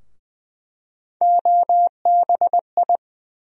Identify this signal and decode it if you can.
Morse code, heard as a keyed tone, about 20 wpm, 700 Hz. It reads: OBI
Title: USA Call Sign Postfixes - 3 Letters 20wpm